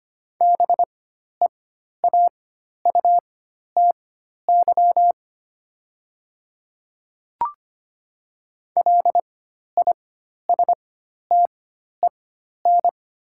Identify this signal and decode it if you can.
Morse code, heard as a keyed tone, about 25 wpm, spaced as for 12 wpm, 700 Hz. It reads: BEAUTY LISTEN
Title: 4x Character Spacing - Top 500 Words - 25wpm